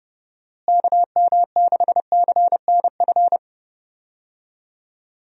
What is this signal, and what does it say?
Morse code, heard as a keyed tone, about 32 wpm, 700 Hz. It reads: KM6CNF
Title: USA Call Signs - 6 Characters - Repeated 2x - 30wpm